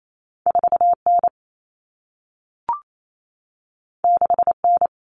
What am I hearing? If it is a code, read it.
Morse code, heard as a keyed tone, about 28 wpm, 700 Hz. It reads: 4N 6N
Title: Common Cut Numbers - 2-Digit 28wpm